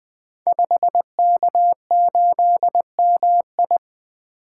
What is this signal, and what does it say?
Morse code, heard as a keyed tone, about 20 wpm, 700 Hz. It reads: 5K8MI
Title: Code Groups - 5 Letter:Numbers - 20wpm